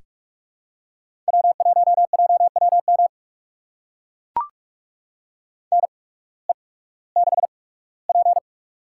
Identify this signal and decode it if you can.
Morse code, heard as a keyed tone, about 44 wpm, spaced as for 21 wpm, 700 Hz. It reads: W1JWM NE6P
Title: USA Call Signs - 8x Character Spacing - 45wpm